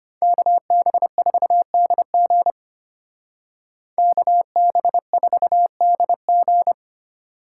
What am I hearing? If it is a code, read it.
Morse code, heard as a keyed tone, about 33 wpm, 700 Hz. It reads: KB4DG KB4DG